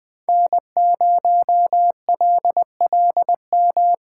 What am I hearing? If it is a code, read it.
Morse code, heard as a keyed tone, about 22 wpm, 700 Hz. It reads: N0LLM
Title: Speed Racing - USA 1x3 Call Signs 20wpm